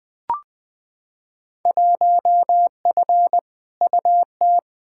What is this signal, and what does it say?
Morse code, heard as a keyed tone, about 20 wpm, 700 Hz. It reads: 1F UT
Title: Field Day Exchanges Encore - 20wpm